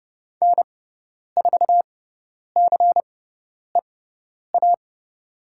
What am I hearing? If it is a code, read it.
Morse code, heard as a keyed tone, about 30 wpm, spaced as for 10 wpm, 690 Hz. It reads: N4CEA
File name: Lesson 17g - Letter u - Call Signs 30wpm